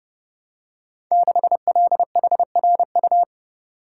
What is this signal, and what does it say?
Morse code, heard as a keyed tone, about 29 wpm, 700 Hz. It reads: 6LHRU
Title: Code Groups - 5 Letter:Numbers - 30wpm